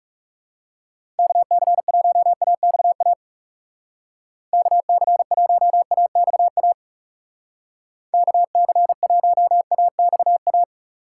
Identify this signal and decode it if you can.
Morse code, heard as a keyed tone, about 36 wpm, 700 Hz. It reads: KC1AXA KC1AXA KC1AXA
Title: Speed Racing - USA 2x3 Call Signs 30wpm